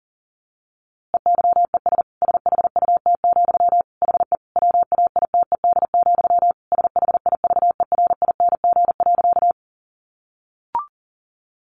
Morse code, heard as a keyed tone, about 37 wpm, 700 Hz. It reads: EYES SHUT, HE WAITED, SHIVERING.
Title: A Sound Of Thunder 40wpm